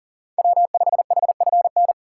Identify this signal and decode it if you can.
Morse code, heard as a keyed tone, about 40 wpm, 700 Hz. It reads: W5HFN
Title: USA Call Signs - 5 Characters - 2x Character Spacing - 40wpm